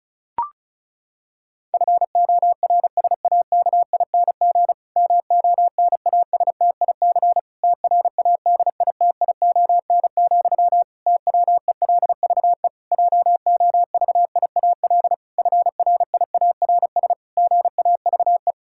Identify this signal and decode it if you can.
Morse code, heard as a keyed tone, about 36 wpm, 700 Hz. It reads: FORSAKING MONASTIC TRADITION, TWELVE JOVIAL FRIARS GAVE
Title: Pangram Challenge v2 - 35wpm